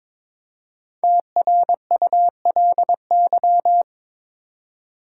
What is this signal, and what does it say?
Morse code, heard as a keyed tone, about 22 wpm, 700 Hz. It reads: TRULY
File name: 5 Letter Words - First 3 Letters Given - 22wpm